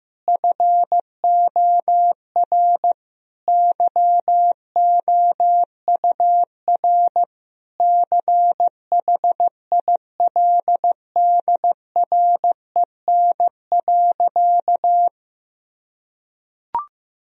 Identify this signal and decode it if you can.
Morse code, heard as a keyed tone, about 15 wpm, 700 Hz. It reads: FOR YOUR CHILDREN.
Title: Sentences from Top 1000 Words - Review Part 3 - 15wpm